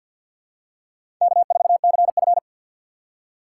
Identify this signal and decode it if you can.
Morse code, heard as a keyed tone, about 50 wpm, 690 Hz. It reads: K4CF
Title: USA Call Signs - 4 Characters - 6x Character Spacing - 50wpm